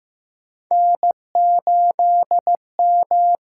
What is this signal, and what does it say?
Morse code, heard as a keyed tone, about 15 wpm, 700 Hz. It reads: N8M
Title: USA Call Signs - 5 Characters - 8x Character Spacing - 15wpm